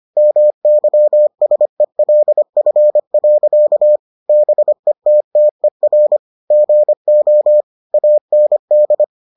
Morse code, heard as a keyed tone, about 25 wpm, 600 Hz. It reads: MYSELF. BETTER GO AND